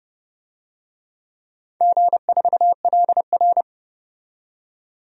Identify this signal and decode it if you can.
Morse code, heard as a keyed tone, about 30 wpm, 700 Hz. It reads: G4LR
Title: DXCC-100 Call Signs - 4 Characters - 30wpm